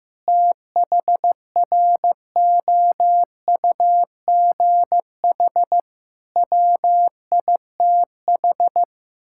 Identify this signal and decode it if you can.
Morse code, heard as a keyed tone, about 15 wpm, 700 Hz. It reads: THROUGH WITH
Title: Sentences from Top 1000 Words - Review Part 6 - 15wpm